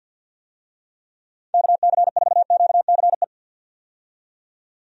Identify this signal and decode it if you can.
Morse code, heard as a keyed tone, about 52 wpm, 700 Hz. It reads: KC4QCE
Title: USA Call Signs - 6 Characters - 50wpm